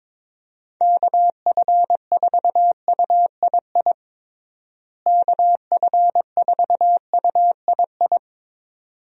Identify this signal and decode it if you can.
Morse code, heard as a keyed tone, about 22 wpm, 700 Hz. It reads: KF4UII KF4UII
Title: USA Call Signs - 6 Characters - Repeated 2x - 22wpm